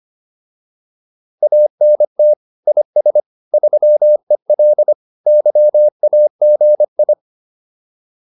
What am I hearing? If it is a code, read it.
Morse code, heard as a keyed tone, about 25 wpm, 600 Hz. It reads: ANT IS 3EL YAGI